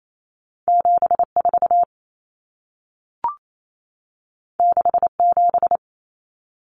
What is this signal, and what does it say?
Morse code, heard as a keyed tone, about 28 wpm, 700 Hz. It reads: 74 67
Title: Numbers - 2-Digit 28wpm